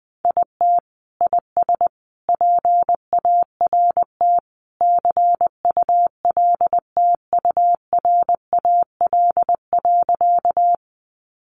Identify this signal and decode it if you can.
Morse code, heard as a keyed tone, about 20 wpm, 700 Hz. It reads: IT IS PART CULTURAL.
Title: Sentences from Top 1500 Words 20wpm